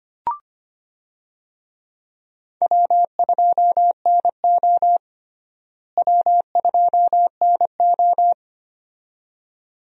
Morse code, heard as a keyed tone, about 25 wpm, 700 Hz. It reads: W2NO W2NO
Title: USA Call Signs - 4 Characters - Repeated 2x - 25wpm